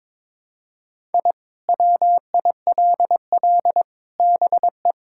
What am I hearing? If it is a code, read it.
Morse code, heard as a keyed tone, about 22 wpm, 700 Hz. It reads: I WILL BE
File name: practice / 22wpm / Sentences with 4-Word Phrases - 22wpm